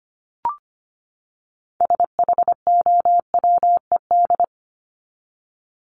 Morse code, heard as a keyed tone, about 25 wpm, 700 Hz. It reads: SHOWED
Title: ED Suffix - Top 100 Words with Base Words 25wpm